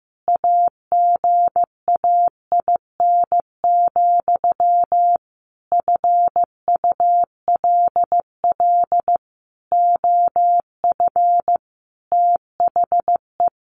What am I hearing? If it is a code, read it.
Morse code, heard as a keyed tone, about 15 wpm, 700 Hz. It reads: AGAIN, FULL OF THE
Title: A Sound Of Thunder 15wpm